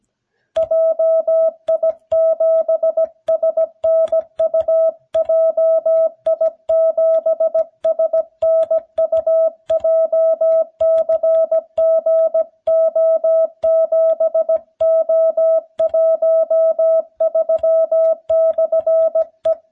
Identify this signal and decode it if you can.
Morse code, heard as a keyed tone, about 17 wpm, 650 Hz. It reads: JI7SNU JI7SNU JCGO7O13/E